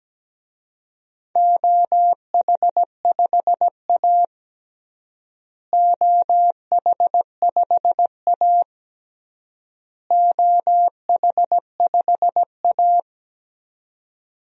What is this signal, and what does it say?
Morse code, heard as a keyed tone, about 17 wpm, 700 Hz. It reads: OH5A OH5A OH5A
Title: DXCC-100 Call Signs - 4 Characters - Repeated 3x - 17wpm